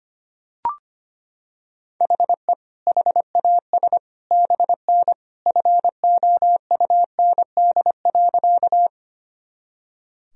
Morse code, heard as a keyed tone, about 25 wpm, 700 Hz. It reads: HE HAS BN FOUND.
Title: Sentences from Top 300 Words and Abbreviations 25wpm